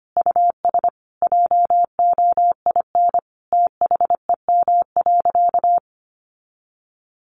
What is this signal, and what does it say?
Morse code, heard as a keyed tone, about 25 wpm, 700 Hz. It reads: US JOIN THEM.
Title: Sentences from Top 1400 Words 25wpm